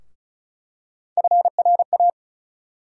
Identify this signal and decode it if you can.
Morse code, heard as a keyed tone, about 34 wpm, 700 Hz. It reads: FRA
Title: USA Call Sign Postfixes - 3 Letters 35wpm